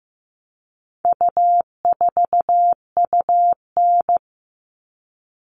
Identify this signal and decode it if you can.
Morse code, heard as a keyed tone, about 15 wpm, 700 Hz. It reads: U4UN
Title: All Cut Numbers - 4-Digit 15wpm